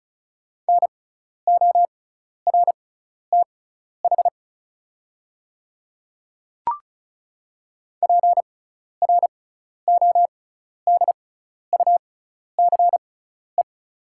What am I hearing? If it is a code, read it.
Morse code, heard as a keyed tone, about 36 wpm, spaced as for 12 wpm, 700 Hz. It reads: NORTH PRODUCE
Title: 6x Character Spacing - Top 500 Words - 35wpm